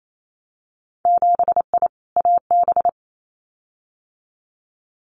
Morse code, heard as a keyed tone, about 28 wpm, 700 Hz. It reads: 7I AB
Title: Winter Field Day Exchanges Encore 28wpm